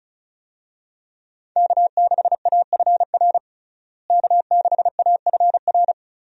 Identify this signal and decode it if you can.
Morse code, heard as a keyed tone, about 36 wpm, 700 Hz. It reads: K6AFR K6AFR
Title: USA Call Signs - 5 Characters - Repeated 3x - 35wpm